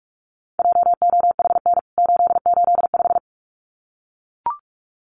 Morse code, heard as a keyed tone, about 44 wpm, 700 Hz. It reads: JOHN 885
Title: CWT Exchange 45wpm